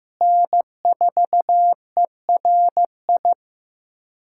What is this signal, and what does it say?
Morse code, heard as a keyed tone, about 15 wpm, 700 Hz. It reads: N4ERI
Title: USA Callsign DE Callsign 15wpm